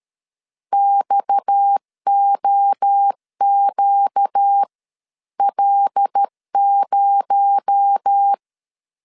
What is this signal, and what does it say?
Morse code, heard as a keyed tone, about 13 wpm, spaced as for 11 wpm, 800 Hz. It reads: XOQL0
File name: CW / DET / 1 6 1500C S A 12wpm 3440